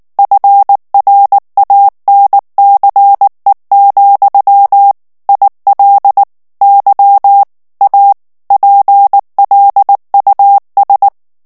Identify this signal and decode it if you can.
Morse code, heard as a keyed tone, about 19 wpm, 800 Hz. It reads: FRANCE, IL Y A PLUS
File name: FRANCAIS TEXTE CLAIR 1 VITESSE 18 MOTS MINUTES